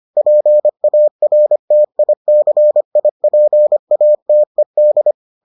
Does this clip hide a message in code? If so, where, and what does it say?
Yes: Morse code from 0.2 to 5.1 s, heard as a keyed tone, about 25 wpm, 600 Hz. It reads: PARTICIPATED